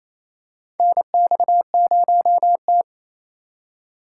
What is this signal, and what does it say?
Morse code, heard as a keyed tone, about 28 wpm, 700 Hz. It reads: NX0T
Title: USA Call Signs - 4 Characters - 8x Character Spacing - 28wpm